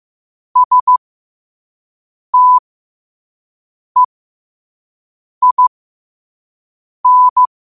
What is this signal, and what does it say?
Morse code, heard as a keyed tone, about 15 wpm, spaced as for 5 wpm, 1000 Hz. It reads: STEIN
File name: qotdfiles / 5wpm